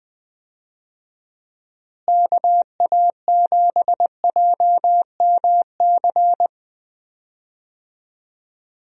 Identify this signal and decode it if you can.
Morse code, heard as a keyed tone, about 21 wpm, 700 Hz. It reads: KA7JMC
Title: USA Call Signs - 6 Characters - Repeated 2x - 20wpm